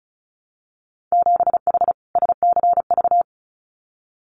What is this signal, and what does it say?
Morse code, heard as a keyed tone, about 36 wpm, 700 Hz. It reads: 7H SCV
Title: Winter Field Day Exchanges Encore 35wpm